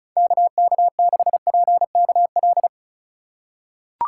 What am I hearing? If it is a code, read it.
Morse code, heard as a keyed tone, about 36 wpm, 700 Hz. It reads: KK6PKL
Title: USA Call Signs - 6 Characters - 4x Character Spacing - 35wpm